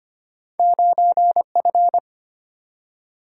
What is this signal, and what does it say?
Morse code, heard as a keyed tone, about 34 wpm, 700 Hz. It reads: 9F